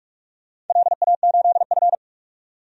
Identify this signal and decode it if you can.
Morse code, heard as a keyed tone, about 44 wpm, 690 Hz. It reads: LA8F
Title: DXCC-100 Call Signs - 4 Characters - 45wpm